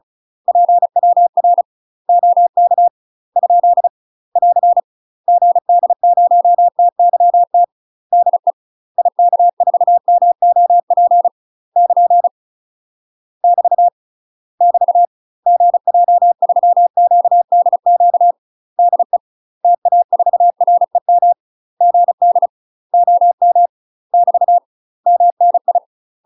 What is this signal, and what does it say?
Morse code, heard as a keyed tone, about 35 wpm, 700 Hz. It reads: PWR OK ? <AR> GD0TYT DE IK4MOP <KN> <BT> <BT> GJ3QDQ DE TA4REM GD OM <BT> MNI
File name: UK MASTER QSOs TRAINING / 35WPM / UK MASTER QSOs 0029